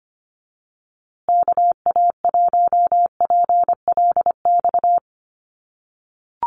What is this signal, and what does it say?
Morse code, heard as a keyed tone, about 25 wpm, 700 Hz. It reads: KA1PLX
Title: USA Call Signs Repeated 3x 25wpm